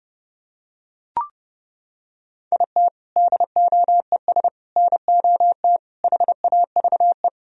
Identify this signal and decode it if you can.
Morse code, heard as a keyed tone, about 31 wpm, 700 Hz. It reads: IT DOES NOT HAVE